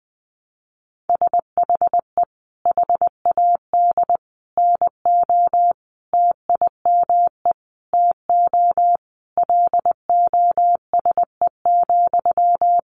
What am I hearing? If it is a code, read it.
Morse code, heard as a keyed tone, about 19 wpm, 700 Hz. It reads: SHE HAD NO TIME TO LOSE,
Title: Sentences from Top 1100 Words 20wpm